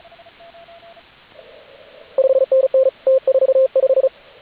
Morse code, heard as a keyed tone, about 35 wpm, 530 Hz, with radio noise. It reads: 5NN T45
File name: AUDIO2019 / HG8DX CW